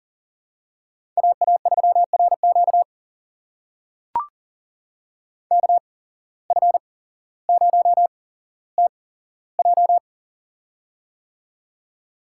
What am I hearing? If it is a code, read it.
Morse code, heard as a keyed tone, about 40 wpm, spaced as for 17 wpm, 700 Hz. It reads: AA3RQ KF0TJ
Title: USA Call Signs - 5 Characters - 8x Character Spacing - 40wpm